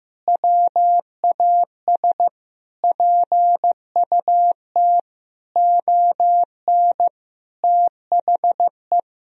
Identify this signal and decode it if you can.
Morse code, heard as a keyed tone, about 15 wpm, 700 Hz. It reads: WAS PUT ON THE
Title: Sentences from Top 1000 Words - Review Part 1 - 15wpm